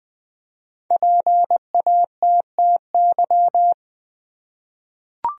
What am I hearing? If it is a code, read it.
Morse code, heard as a keyed tone, about 20 wpm, 700 Hz. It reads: PATTY
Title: USA Common Names - Max 5 Letters - 20wpm